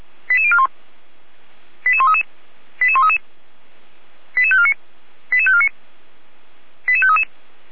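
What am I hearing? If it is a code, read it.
Five-tone selective call (ZVEI), heard as quick runs of five tones, tones of about 70 ms. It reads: 88041 88133 88133 88549 88549 88533
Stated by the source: ZVEI five-tone selective call received by SDR.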